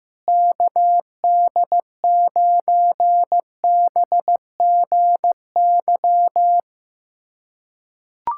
Morse code, heard as a keyed tone, about 16 wpm, 700 Hz. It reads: KD9BGY
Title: USA Callsign DE Callsign - 6 Characters - 15wpm